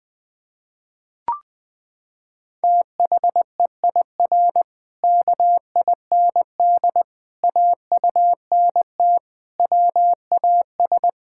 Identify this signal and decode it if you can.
Morse code, heard as a keyed tone, about 20 wpm, 700 Hz. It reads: THEIR KIND AUNT WAS
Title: Pangram Challenge v2 - 20wpm